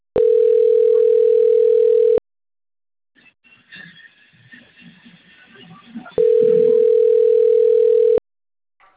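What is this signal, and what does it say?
Ringback tone (North America), 440 + 480 Hz.
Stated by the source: Telephone line recorded at Cornelio Procopio.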